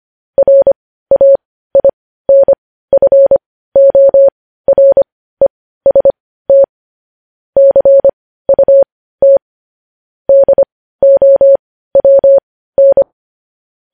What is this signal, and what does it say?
Morse code, heard as a keyed tone, about 25 wpm, spaced as for 15 wpm, 550 Hz. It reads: RAINFOREST CUT DOWN
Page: news headlines in morse code at 15 wpm